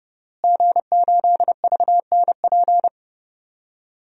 Morse code, heard as a keyed tone, about 30 wpm, 700 Hz. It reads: G8VNP
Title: DXCC-100 Call Signs - 5 Characters - Repeated 3x - 30wpm